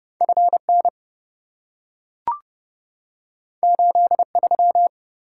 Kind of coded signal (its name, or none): Morse code